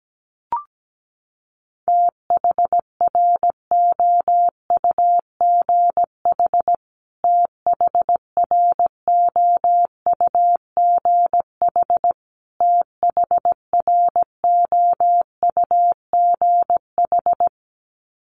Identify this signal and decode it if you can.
Morse code, heard as a keyed tone, about 17 wpm, 700 Hz. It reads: THROUGH THROUGH THROUGH
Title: Top 300 Words Repeated 17wpm